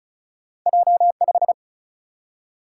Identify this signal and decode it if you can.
Morse code, heard as a keyed tone, about 36 wpm, 700 Hz. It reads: J 5
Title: Single-Letter Single-Number 35wpm